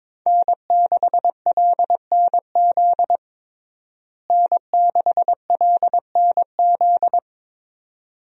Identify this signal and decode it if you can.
Morse code, heard as a keyed tone, about 22 wpm, 700 Hz. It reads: N6LNZ N6LNZ
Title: USA Call Signs - 5 Characters - Repeated 2x - 22wpm